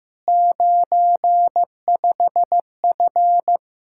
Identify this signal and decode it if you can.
Morse code, heard as a keyed tone, about 15 wpm, 700 Hz. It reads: 95F
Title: K1USN SST QSO Practice 15wpm